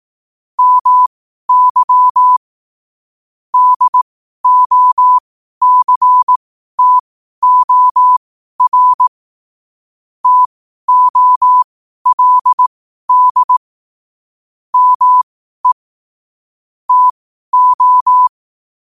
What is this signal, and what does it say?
Morse code, heard as a keyed tone, about 18 wpm, spaced as for 12 wpm, 1000 Hz. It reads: MY DOCTOR TOLD ME TO
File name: qotdfiles / 12wpm